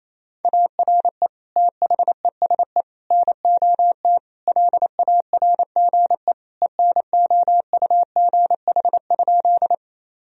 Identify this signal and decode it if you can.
Morse code, heard as a keyed tone, about 28 wpm, 700 Hz. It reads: ARE THESE NOT LARGE ENOUGH?
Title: Sentences from Top 1000 Words - Review Part 1 - 28wpm